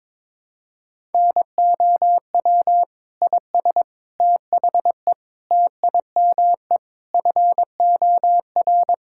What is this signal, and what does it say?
Morse code, heard as a keyed tone, about 22 wpm, 700 Hz. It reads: NOW IS THE TIME FOR
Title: Pangram Challenge v2 - 22wpm